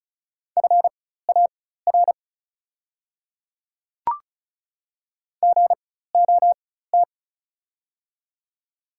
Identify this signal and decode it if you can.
Morse code, heard as a keyed tone, about 36 wpm, spaced as for 16 wpm, 700 Hz. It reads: FAR GOT
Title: 4x Character Spacing - Top 300 Words - 35wpm